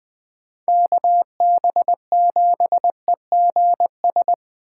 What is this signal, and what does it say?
Morse code, heard as a keyed tone, about 20 wpm, 700 Hz. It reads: KB7EGS
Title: USA Call Signs - 6 Characters - 2x Character Spacing - 20wpm